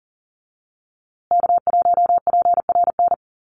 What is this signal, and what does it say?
Morse code, heard as a keyed tone, about 40 wpm, 700 Hz. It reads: K1PRN
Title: USA Call Signs Repeated 2x 40wpm